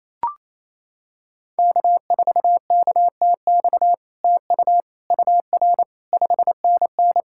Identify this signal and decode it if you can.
Morse code, heard as a keyed tone, about 28 wpm, 700 Hz. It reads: K4KTX TU UR 5NN
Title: SOTA QSO Practice 28wpm